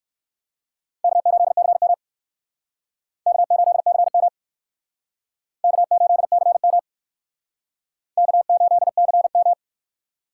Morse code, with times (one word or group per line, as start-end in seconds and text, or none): 1.0-1.9 K8CM
3.3-4.3 K8CM
5.6-6.8 K8CM
8.2-9.5 K8CM